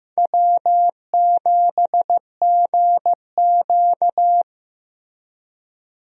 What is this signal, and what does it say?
Morse code, heard as a keyed tone, about 15 wpm, 700 Hz. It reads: W7GQ
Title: USA Call Signs - 4 Characters - Repeated 3x - 15wpm